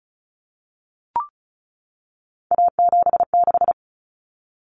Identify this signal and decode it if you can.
Morse code, heard as a keyed tone, about 34 wpm, 700 Hz. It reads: A76
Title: Common Cut Numbers - 3-Digit 35wpm